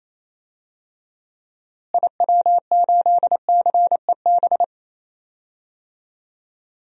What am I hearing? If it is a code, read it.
Morse code, heard as a keyed tone, about 28 wpm, 700 Hz. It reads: IW8CEB
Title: DXCC-100 Call Signs - 6 Characters - Repeated 2x - 28wpm